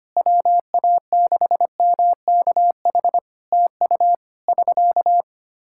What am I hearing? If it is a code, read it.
Morse code, heard as a keyed tone, about 25 wpm, 700 Hz. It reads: WA6MKH TU <SK>
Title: K1USN SST QSO Practice 25wpm